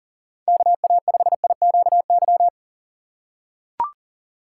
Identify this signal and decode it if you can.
Morse code, heard as a keyed tone, about 40 wpm, 700 Hz. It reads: KA5IQY
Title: USA Call Signs - 6 Characters - 6x Character Spacing - 40wpm